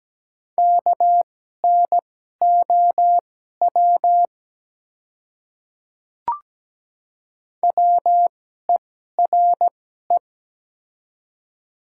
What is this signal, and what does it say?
Morse code, heard as a keyed tone, about 17 wpm, spaced as for 12 wpm, 690 Hz. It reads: KNOW WERE
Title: 2x Character Spacing - Top 300 Words - 17wpm